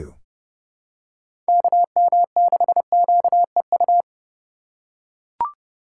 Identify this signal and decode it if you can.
Morse code, heard as a keyed tone, about 30 wpm, 700 Hz. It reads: KM6QEU
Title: USA Call Signs - 3x Character Spacing - 30wpm